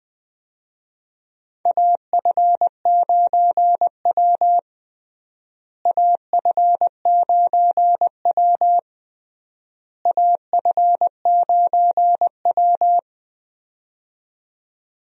Morse code, heard as a keyed tone, about 20 wpm, 700 Hz. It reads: AF9W AF9W AF9W
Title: USA Call Signs - 4 Characters - Repeated 3x - 20wpm